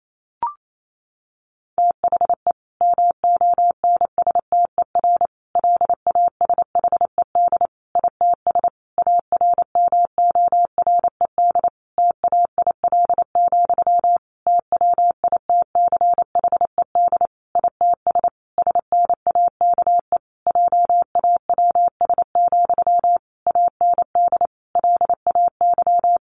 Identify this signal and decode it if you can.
Morse code, heard as a keyed tone, about 28 wpm, 700 Hz. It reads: THE MONSTER LASHED ITS ARMORED TAIL, TWITCHED ITS SNAKE JAWS, AND LAY
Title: A Sound Of Thunder 28wpm